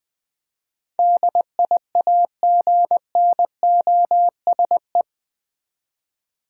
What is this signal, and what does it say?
Morse code, heard as a keyed tone, about 21 wpm, 700 Hz. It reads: DIAGNOSE
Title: Eight Letter Words - 20wpm